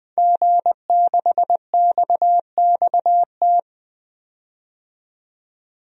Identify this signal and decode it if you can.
Morse code, heard as a keyed tone, about 20 wpm, 700 Hz. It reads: G6XXT
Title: DXCC-100 Call Signs - 5 Characters - 20wpm